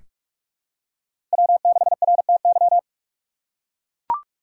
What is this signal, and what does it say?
Morse code, heard as a keyed tone, about 41 wpm, 700 Hz. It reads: W6RTY
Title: USA Call Signs - 5 Characters - 4x Character Spacing - 45wpm